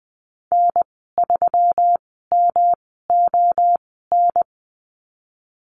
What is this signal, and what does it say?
Morse code, heard as a keyed tone, about 20 wpm, spaced as for 14 wpm, 700 Hz. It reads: N3MON
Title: 179 Lesson 29g - Slash - Call Signs 20wpm